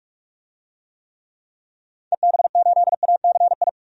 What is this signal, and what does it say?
Morse code, heard as a keyed tone, about 44 wpm, 700 Hz. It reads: EB8ACI
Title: DXCC-100 Call Signs - 6 Characters - 45wpm